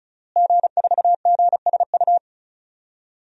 Morse code, heard as a keyed tone, about 36 wpm, 700 Hz. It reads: G4GSU
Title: DXCC-100 Call Signs - 5 Characters - Repeated 3x - 35wpm